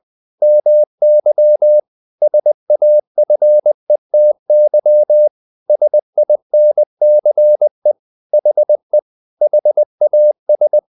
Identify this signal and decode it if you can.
Morse code, heard as a keyed tone, about 20 wpm, 600 Hz. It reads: MY SAFETY SINCE HE HAS
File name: the old man and the sea part 2 20wpm